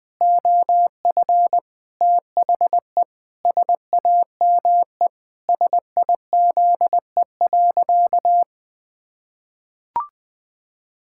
Morse code, heard as a keyed tone, about 20 wpm, 700 Hz. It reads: OF THE SAME SIZE.
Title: Pangram Challenge v2 - 20wpm